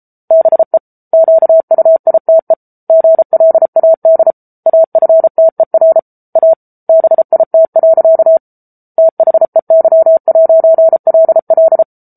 Morse code, heard as a keyed tone, about 34 wpm, 650 Hz. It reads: BE QUITE GLAD AFTER A BIT. THEY'LL
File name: Book Two / 07-Chapter 7 - 33 WPM